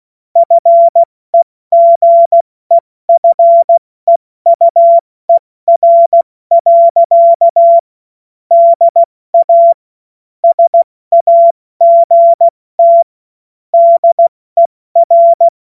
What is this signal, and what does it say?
Morse code, heard as a keyed tone, about 16 wpm, 680 Hz. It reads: FEGEFEUER. DA SAGT DER
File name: Morsekurs / Witze / Witz 09. Dezember 2025 70 F80